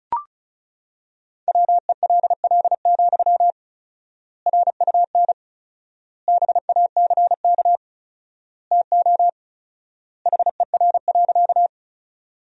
Morse code, heard as a keyed tone, about 36 wpm, 700 Hz. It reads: WELL, RUN BACK TO HER.
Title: 4x Word-Spacing - Sentences from Top 300 Words - 35wpm